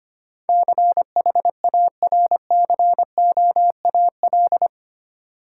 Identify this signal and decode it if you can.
Morse code, heard as a keyed tone, about 25 wpm, 690 Hz. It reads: CHARCOAL
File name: practice / 25wpm / Easy Words with Hints - 25wpm